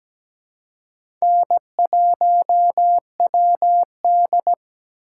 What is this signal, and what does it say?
Morse code, heard as a keyed tone, about 17 wpm, 700 Hz. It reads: N1WD
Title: USA Call Signs - 4 Characters - 2x Character Spacing - 17wpm